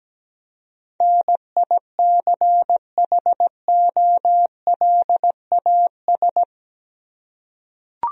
Morse code, heard as a keyed tone, about 17 wpm, 700 Hz. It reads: NICHOLAS
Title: USA Common Names - 17wpm